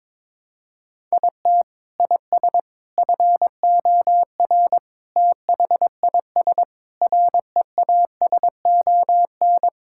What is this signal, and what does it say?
Morse code, heard as a keyed tone, about 22 wpm, 700 Hz. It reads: IT IS FOR THIS REASON
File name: practice / 22wpm / Sets of 6 Words - 22wpm